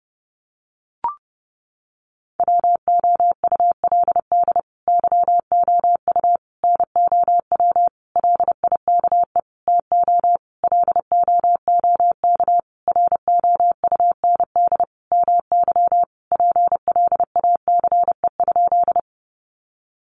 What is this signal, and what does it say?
Morse code, heard as a keyed tone, about 30 wpm, 700 Hz. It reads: WOULD YOU NOW LIKE TO LOOK ROUND MY PLACE?
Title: Sentences from Top 200 Words 30wpm